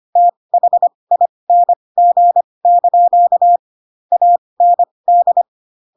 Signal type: Morse code